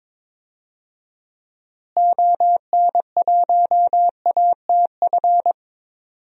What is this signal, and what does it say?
Morse code, heard as a keyed tone, about 21 wpm, 700 Hz. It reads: ON1ATF
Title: DXCC-100 Call Signs - 6 Characters - Repeated 3x - 22wpm